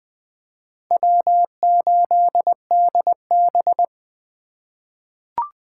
Morse code, heard as a keyed tone, about 20 wpm, 700 Hz. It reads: W8DB
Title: USA Call Signs - 4 Characters - 3x Character Spacing - 20wpm